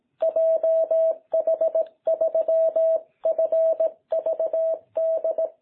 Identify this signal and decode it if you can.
Morse code, heard as a keyed tone, about 17 wpm, 660 Hz. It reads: JH3FVD
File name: jh8cbh / 2021cw